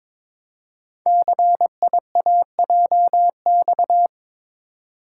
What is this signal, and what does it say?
Morse code, heard as a keyed tone, about 22 wpm, 700 Hz. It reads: CIAJX
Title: Code Groups - 5 Letters - 22wpm